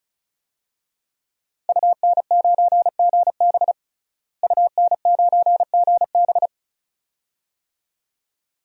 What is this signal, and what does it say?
Morse code, heard as a keyed tone, about 34 wpm, 700 Hz. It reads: UN9GB UN9GB
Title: DXCC-100 Call Signs - 5 Characters - Repeated 2x - 35wpm